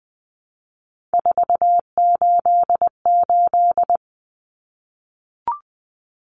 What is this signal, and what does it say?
Morse code, heard as a keyed tone, about 20 wpm, 700 Hz. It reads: 488
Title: Numbers - 3-Digit 20wpm